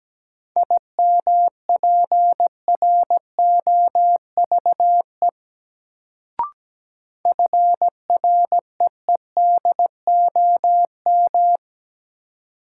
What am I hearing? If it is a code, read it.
Morse code, heard as a keyed tone, about 17 wpm, 700 Hz. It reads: IMPROVE FREEDOM
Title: Sentences from Top 800 Words - New Word Drill - 17wpm